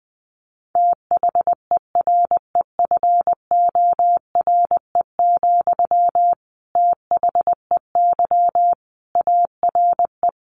Morse code, heard as a keyed tone, about 19 wpm, 700 Hz. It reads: THEREFORE, THEY ARE
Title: Sentences from Top 1200 Words 20wpm